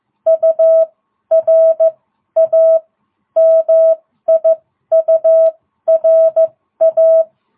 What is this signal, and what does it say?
Morse code, heard as a keyed tone, about 15 wpm, spaced as for 12 wpm, 660 Hz. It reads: URA MIURA